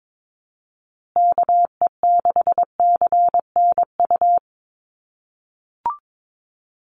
Morse code, heard as a keyed tone, about 22 wpm, 700 Hz. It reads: KE6CNU
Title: USA Call Signs Repeated 2x 22wpm